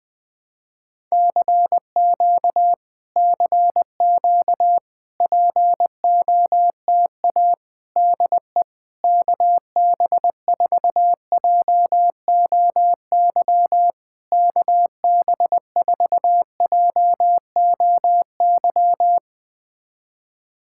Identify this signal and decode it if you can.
Morse code, heard as a keyed tone, about 20 wpm, 700 Hz. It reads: CQ CQ POTA DE KB4JOY KB4JOY
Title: POTA QSO Practice - 20wpm